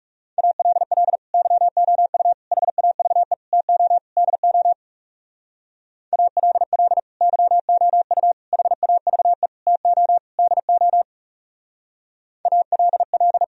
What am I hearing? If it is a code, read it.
Morse code, heard as a keyed tone, about 39 wpm, 700 Hz. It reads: ALL YOU HAVE TO DO ALL YOU HAVE TO DO ALL